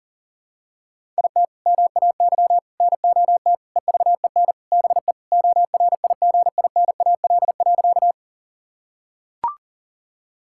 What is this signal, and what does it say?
Morse code, heard as a keyed tone, about 40 wpm, 700 Hz. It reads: IT MAY NOT EVEN BE ORIGINAL.
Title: Sentences from Top 1000 Words - Review Part 6 - 40wpm